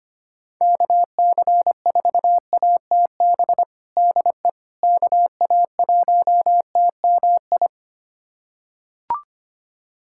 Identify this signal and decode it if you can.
Morse code, heard as a keyed tone, about 25 wpm, 700 Hz. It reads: KC4ATB DE KA1TMI
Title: USA Callsign DE Callsign - 6 Characters - 25wpm